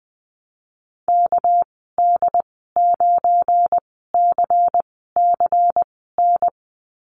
Morse code, heard as a keyed tone, about 20 wpm, spaced as for 15 wpm, 700 Hz. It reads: KD9CCN
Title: 200 Lesson 32g - Letter k - Call Signs 20wpm